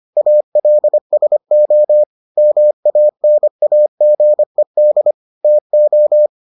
Morse code, heard as a keyed tone, about 25 wpm, 600 Hz. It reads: ALSO MANAGED TO